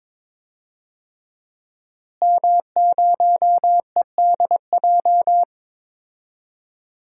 Morse code, heard as a keyed tone, about 22 wpm, 700 Hz. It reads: M0EDJ
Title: DXCC-100 Call Signs - 5 Characters - Repeated 3x - 22wpm